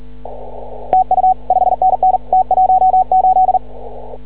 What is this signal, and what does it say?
Morse code, heard as a keyed tone, about 40 wpm, 750 Hz. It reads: TU 5NN T18
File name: CW / ha6oi 2